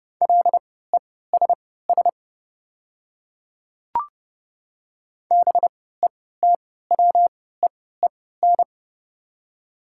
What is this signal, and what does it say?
Morse code, heard as a keyed tone, about 30 wpm, spaced as for 17 wpm, 690 Hz. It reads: LESS BETWEEN
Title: 3x Character Spacing - Top 500 Words - 30wpm